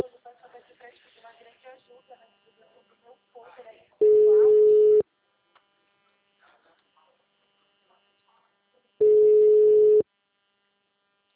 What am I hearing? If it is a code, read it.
Ringback tone (Europe, 425 Hz), with speech.